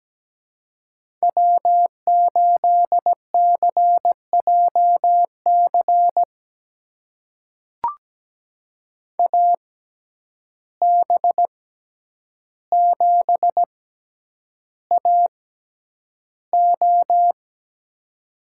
Morse code, heard as a keyed tone, about 17 wpm, spaced as for 9 wpm, 700 Hz. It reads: W8CJC AB7AO
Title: USA Call Signs - 5 Characters - 6x Character Spacing - 17wpm